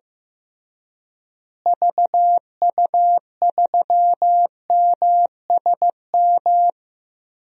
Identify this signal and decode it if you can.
Morse code, heard as a keyed tone, about 15 wpm, 700 Hz. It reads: VU3MSM
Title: DXCC-100 Call Signs - 6 Characters - Repeated 3x - 15wpm